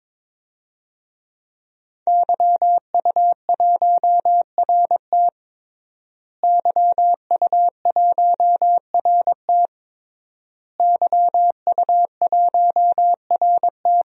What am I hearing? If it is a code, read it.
Morse code, heard as a keyed tone, about 22 wpm, 700 Hz. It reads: YU1RT YU1RT YU1RT
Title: DXCC-100 Call Signs - 5 Characters - Repeated 3x - 22wpm